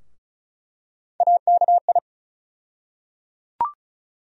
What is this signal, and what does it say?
Morse code, heard as a keyed tone, about 34 wpm, 700 Hz. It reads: AKI
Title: USA Call Sign Postfixes - 3 Letters 35wpm